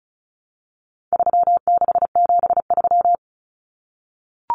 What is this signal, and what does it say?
Morse code, heard as a keyed tone, about 34 wpm, 700 Hz. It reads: 3673
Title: Common Cut Numbers - 4-Digit 35wpm